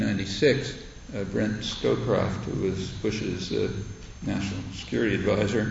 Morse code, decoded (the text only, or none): none